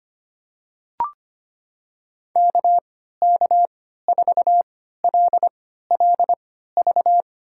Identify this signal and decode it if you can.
Morse code, heard as a keyed tone, about 25 wpm, spaced as for 14 wpm, 700 Hz. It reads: KK4LLV
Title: USA Call Signs - 3x Character Spacing - 25wpm